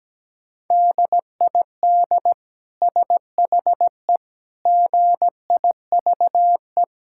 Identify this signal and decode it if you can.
Morse code, heard as a keyed tone, about 17 wpm, 700 Hz. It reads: DID SHE GIVE
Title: Sentences from Top 1000 Words - Review Part 5 - 17wpm